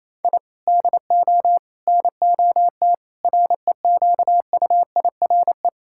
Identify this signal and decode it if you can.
Morse code, heard as a keyed tone, about 28 wpm, 700 Hz. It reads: I DO NOT REQUIRE
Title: Sentences from Top 1000 Words - Review Part 6 - 28wpm